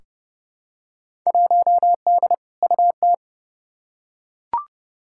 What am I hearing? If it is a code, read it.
Morse code, heard as a keyed tone, about 31 wpm, 700 Hz. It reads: 1D UT
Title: Field Day Exchanges - 30wpm